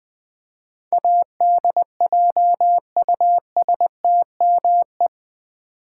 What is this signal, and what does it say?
Morse code, heard as a keyed tone, about 19 wpm, 700 Hz. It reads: ADJUSTME
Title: Build A Word - 4 Letters At A Time - 10-Letter Words - 20wpm